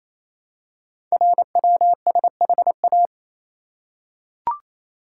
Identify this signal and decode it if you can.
Morse code, heard as a keyed tone, about 27 wpm, 700 Hz. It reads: RWSHA
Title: Code Groups - 5 Letters - 28wpm